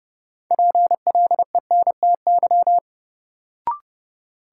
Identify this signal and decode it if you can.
Morse code, heard as a keyed tone, about 31 wpm, 700 Hz. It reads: PLENTY
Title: Sentences from Top 1300 Words - New Word Drill Alt - 30wpm